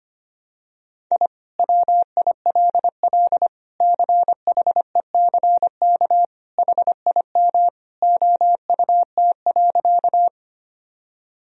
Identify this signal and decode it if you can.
Morse code, heard as a keyed tone, about 25 wpm, 700 Hz. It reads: I WILL CHECK HIM OUT.
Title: Sentences from Top 1000 Words - Review Part 3 - 25wpm